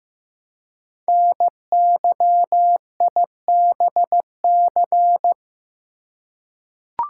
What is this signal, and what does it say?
Morse code, heard as a keyed tone, about 15 wpm, 700 Hz. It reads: NYIBC
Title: Code Groups - 5 Letter:Numbers - 15wpm